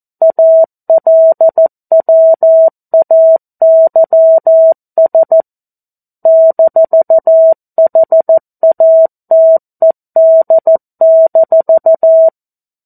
Morse code, heard as a keyed tone, about 14 wpm, 650 Hz. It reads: ALWAYS -HATED-
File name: 02-Chapter-02 14 WPM